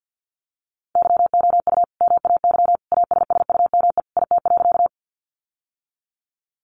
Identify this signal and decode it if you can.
Morse code, heard as a keyed tone, about 50 wpm, 700 Hz. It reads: YOU MAY ASSUME IT.
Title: Sentences from Top 1500 Words 50wpm